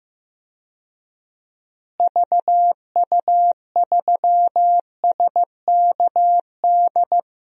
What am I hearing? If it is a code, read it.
Morse code, heard as a keyed tone, about 16 wpm, 700 Hz. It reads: VU3SKD
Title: DXCC-100 Call Signs - 6 Characters - Repeated 2x - 15wpm